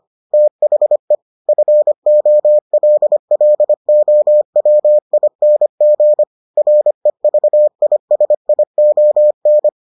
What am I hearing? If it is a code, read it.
Morse code, heard as a keyed tone, about 25 wpm, 600 Hz. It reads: THE FOLLOWING REVISION